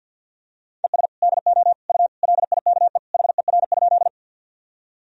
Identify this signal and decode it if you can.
Morse code, heard as a keyed tone, about 47 wpm, 700 Hz. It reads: ES DO U LIKE HER?
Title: Sentences from Top 100 Words and Abbreviations 50wpm